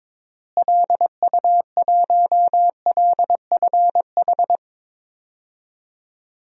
Morse code, heard as a keyed tone, about 22 wpm, 700 Hz. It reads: LU1LFH
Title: DXCC-100 Call Signs - 6 Characters - Repeated 3x - 22wpm